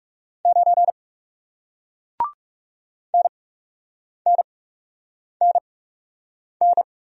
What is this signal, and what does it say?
Morse code, heard as a keyed tone, about 39 wpm, spaced as for 8 wpm, 700 Hz. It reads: 9 NNNN